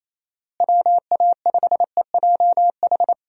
Morse code, heard as a keyed tone, about 28 wpm, 700 Hz. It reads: WA5EJH
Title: USA Call Signs - 6 Characters - Repeated 2x - 28wpm